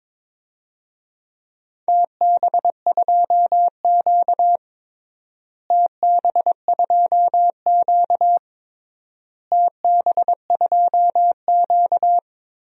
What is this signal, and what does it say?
Morse code, heard as a keyed tone, about 21 wpm, 700 Hz. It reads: TB2Q TB2Q TB2Q
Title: DXCC-100 Call Signs - 4 Characters - Repeated 3x - 22wpm